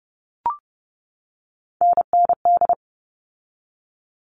Morse code, heard as a keyed tone, about 30 wpm, 700 Hz. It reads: NND
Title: All Cut Numbers - 3-Digit 30wpm